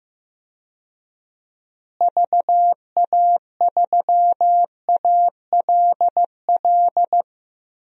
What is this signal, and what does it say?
Morse code, heard as a keyed tone, about 15 wpm, 700 Hz. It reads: VA3ALL
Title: DXCC-100 Call Signs - 6 Characters - Repeated 2x - 15wpm